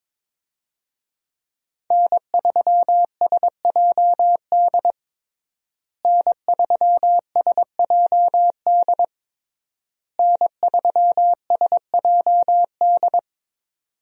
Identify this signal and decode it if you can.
Morse code, heard as a keyed tone, about 22 wpm, 700 Hz. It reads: N3SJD N3SJD N3SJD
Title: USA Call Signs - 5 Characters - Repeated 3x - 22wpm